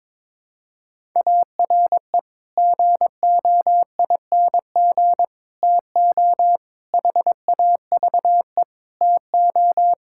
Morse code, heard as a keyed tone, about 22 wpm, 700 Hz. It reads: ARE GOING TO HAVE TO
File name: practice / 22wpm / Build A Phrase - 5-Word Phrases - 22wpm